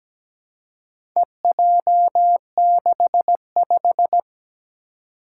Morse code, heard as a keyed tone, about 17 wpm, 700 Hz. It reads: EJ65
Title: Four Challenging Letter-Number Combinations 17wpm